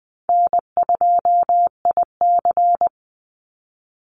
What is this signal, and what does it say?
Morse code, heard as a keyed tone, about 20 wpm, 700 Hz. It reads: N2IC
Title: Call Signs Repeated 2x - 20wpm